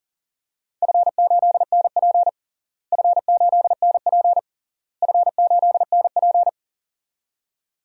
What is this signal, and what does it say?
Morse code, heard as a keyed tone, about 40 wpm, 700 Hz. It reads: F8NP F8NP F8NP
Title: DXCC-100 Call Signs - 4 Characters - Repeated 3x - 40wpm